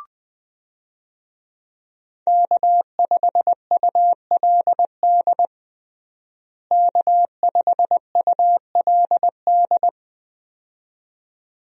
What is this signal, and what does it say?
Morse code, heard as a keyed tone, about 20 wpm, 700 Hz. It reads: K5ULD K5ULD
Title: USA Call Signs - 5 Characters - Repeated 2x - 20wpm